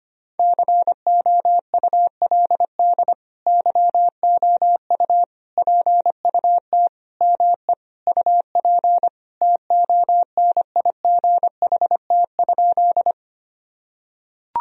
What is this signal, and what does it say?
Morse code, heard as a keyed tone, about 25 wpm, 700 Hz. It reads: COULD YOU PUT ME UP TONIGHT?
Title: Sentences from Top 1000 Words - Review Part 3 - 25wpm